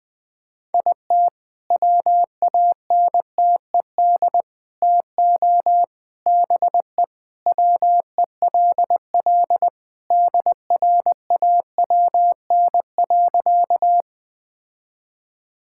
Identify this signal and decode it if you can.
Morse code, heard as a keyed tone, about 20 wpm, 700 Hz. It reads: IT WANTED TO BE WELL DRAWN.
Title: All-Age Appropriate Jokes - 20wpm